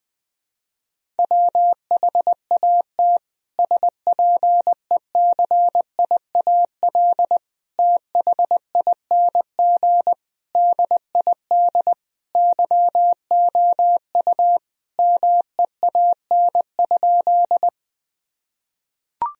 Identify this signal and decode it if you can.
Morse code, heard as a keyed tone, about 21 wpm, 700 Hz. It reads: WHAT SPECIAL THING DID YOU MEAN?
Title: Sentences from Top 1000 Words - Review Part 3 - 20wpm